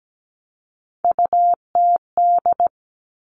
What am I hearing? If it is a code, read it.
Morse code, heard as a keyed tone, about 17 wpm, 700 Hz. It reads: UTD
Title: Three Challenging Letter-Number Combinations 17wpm